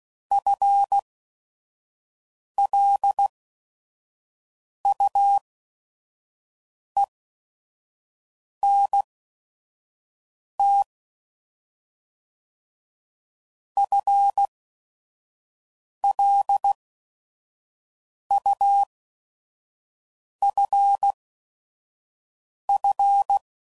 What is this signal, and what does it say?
Morse code, heard as a keyed tone, about 16 wpm, spaced as for 5 wpm, 800 Hz. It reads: FLUENT FLUFF